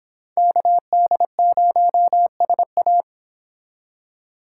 Morse code, heard as a keyed tone, about 25 wpm, 700 Hz. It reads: KD0SA